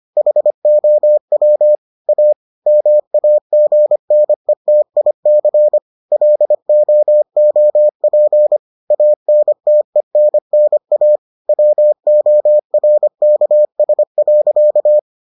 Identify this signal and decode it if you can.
Morse code, heard as a keyed tone, about 25 wpm, 600 Hz. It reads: HOW A MAGNETIC LOOP ANTENNA WORKS.